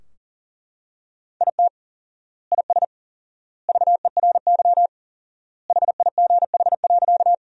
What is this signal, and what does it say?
Morse code, heard as a keyed tone, about 40 wpm, 700 Hz. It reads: IT IS VERY HIGH.
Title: 4x Word-Spacing - Sentences from Top 200 Words - 40wpm